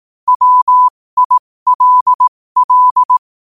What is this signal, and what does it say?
Morse code, heard as a keyed tone, about 18 wpm, spaced as for 15 wpm, 1000 Hz. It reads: WILL
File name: qotdfiles / 15wpm